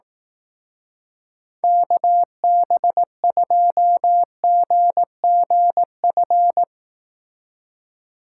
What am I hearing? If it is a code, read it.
Morse code, heard as a keyed tone, about 18 wpm, 700 Hz. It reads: KB2GGF
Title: Speed Racing - USA 2x3 Call Signs 15wpm